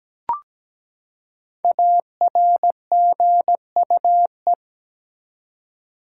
Morse code, heard as a keyed tone, about 17 wpm, 700 Hz. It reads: ARGUE
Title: Five Letter Words - 17wpm